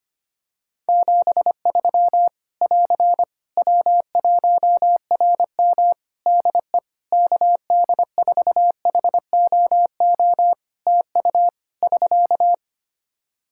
Morse code, heard as a keyed tone, about 25 wpm, 700 Hz. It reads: 73 <AR> W1RM DE KD4HOO TU <SK>
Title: Practice QSO - Exchange 4 - Ending - 25wpm